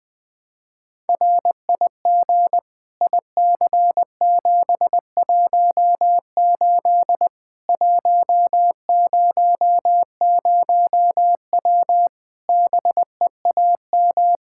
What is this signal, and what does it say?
Morse code, heard as a keyed tone, about 20 wpm, 700 Hz. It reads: RIG IC718 100W BEAM
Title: Practice QSO - Exchange 2 - Info - 20wpm